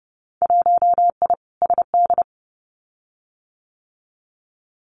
Morse code, heard as a keyed tone, about 30 wpm, 700 Hz. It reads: 1I SD
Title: Winter Field Day Exchanges 30wpm